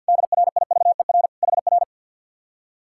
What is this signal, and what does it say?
Morse code, heard as a keyed tone, about 67 wpm, spaced as for 58 wpm, 700 Hz. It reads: DRIVER HR
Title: Speed-Racing Common QSO Phrases Encore - 50wpm